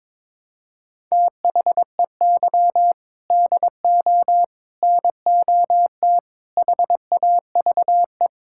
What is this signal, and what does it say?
Morse code, heard as a keyed tone, about 22 wpm, 700 Hz. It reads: THEY DO NOT HAVE
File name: practice / 22wpm / Sentences with 4-Word Phrases - 22wpm